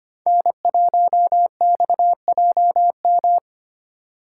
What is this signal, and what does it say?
Morse code, heard as a keyed tone, about 25 wpm, 700 Hz. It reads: N1XJM
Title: K1USN SST QSO Practice - 25wpm